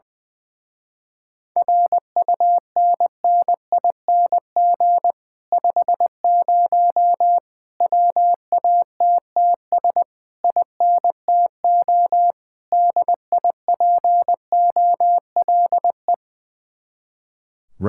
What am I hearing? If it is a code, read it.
Morse code, heard as a keyed tone, about 19 wpm, 700 Hz. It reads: RUNNING 50 WATTS INTO DIPOLE
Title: Speed-Racing Common QSO Phrases - 20wpm